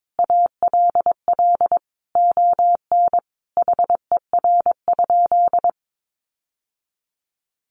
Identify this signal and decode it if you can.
Morse code, heard as a keyed tone, about 22 wpm, 700 Hz. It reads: ALL ON HER?
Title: Sentences from Top 800 Words 22wpm